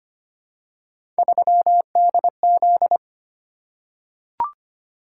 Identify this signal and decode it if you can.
Morse code, heard as a keyed tone, about 25 wpm, 700 Hz. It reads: 3DZ
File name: Three Letter-Number Combinations - 25wpm